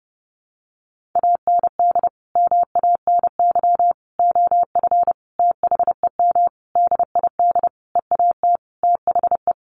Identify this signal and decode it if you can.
Morse code, heard as a keyed tone, about 30 wpm, 700 Hz. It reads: AND MANY OF THEM DID EAT THE
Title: Sentences from Top 1500 Words 30wpm